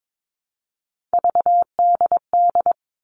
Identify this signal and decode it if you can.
Morse code, heard as a keyed tone, about 22 wpm, 700 Hz. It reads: VDD
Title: All Cut Numbers - 3-Digit 22wpm